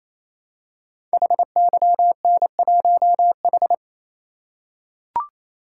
Morse code, heard as a keyed tone, about 27 wpm, 700 Hz. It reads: HYN1H
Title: Code Groups - 5 Letter:Numbers - 28wpm